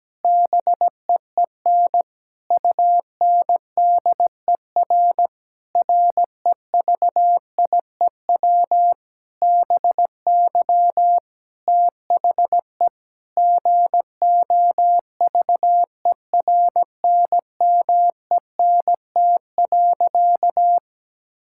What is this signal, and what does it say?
Morse code, heard as a keyed tone, about 17 wpm, 700 Hz. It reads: BEEN UNDER REVIEW BY THE GOVERNMENT.
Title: Sentences from Top 1000 Words - Review Part 6 - 17wpm